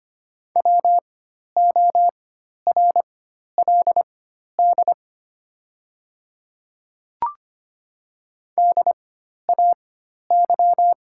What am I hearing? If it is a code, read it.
Morse code, heard as a keyed tone, about 25 wpm, spaced as for 12 wpm, 700 Hz. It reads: WORLD DAY
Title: 4x Character Spacing - Top 200 Words - 25wpm